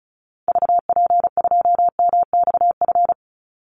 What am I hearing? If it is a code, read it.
Morse code, heard as a keyed tone, about 34 wpm, 700 Hz. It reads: VP2MXF
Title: Call Signs Repeated 3x - 35wpm